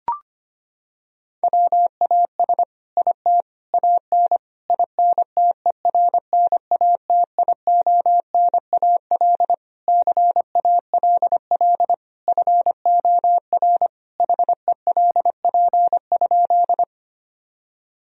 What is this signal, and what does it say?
Morse code, heard as a keyed tone, about 25 wpm, 700 Hz. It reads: WAS IT AN INTERNATIONAL CALL FOR HELP?
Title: Sentences from Top 1000 Words - Review Part 4 - 25wpm